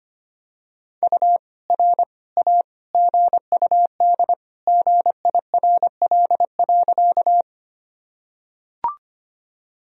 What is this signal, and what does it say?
Morse code, heard as a keyed tone, about 25 wpm, 700 Hz. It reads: U R A GUD GIRL.
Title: Sentences from Top 400 Words and Abbreviations 25wpm